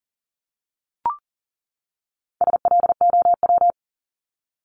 Morse code, heard as a keyed tone, about 40 wpm, 700 Hz. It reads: SLOW
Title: LY Suffix - Top 100 Words with Base Words 40wpm